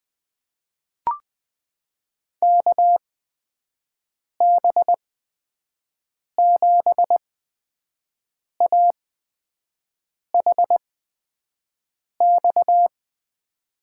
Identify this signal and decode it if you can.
Morse code, heard as a keyed tone, about 20 wpm, spaced as for 5 wpm, 700 Hz. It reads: KB7AHX
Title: USA Call Signs - 8x Character Spacing - 20wpm